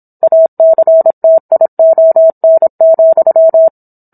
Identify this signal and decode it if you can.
Morse code, heard as a keyed tone, about 27 wpm, 650 Hz. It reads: ACTION,